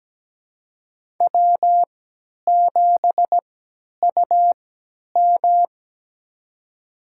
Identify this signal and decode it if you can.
Morse code, heard as a keyed tone, about 17 wpm, spaced as for 10 wpm, 700 Hz. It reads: W7UM
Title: USA Call Signs - 3x Character Spacing - 17wpm